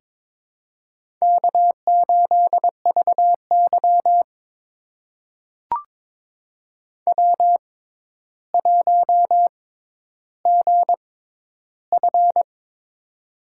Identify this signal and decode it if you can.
Morse code, heard as a keyed tone, about 22 wpm, spaced as for 11 wpm, 700 Hz. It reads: K8VY W1GF
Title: USA Call Signs - 4 Characters - 6x Character Spacing - 22wpm